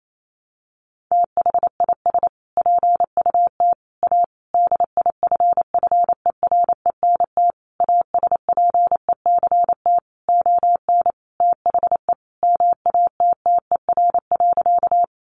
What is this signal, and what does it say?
Morse code, heard as a keyed tone, about 28 wpm, 700 Hz. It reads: THIS PUT A DIFFERENT ASPECT ON THE MATTER.
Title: Sentences from Top 1300 Words 28wpm